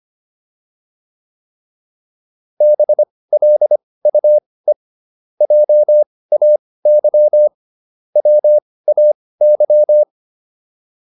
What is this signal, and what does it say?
Morse code, heard as a keyed tone, about 25 wpm, spaced as for 18 wpm, 600 Hz. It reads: BLUE JAY WAY
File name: PR305 18